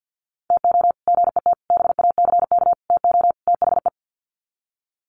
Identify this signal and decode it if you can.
Morse code, heard as a keyed tone, about 48 wpm, 700 Hz. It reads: TO GET BACK TO THE
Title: Sets of 5 Words Encore 50wpm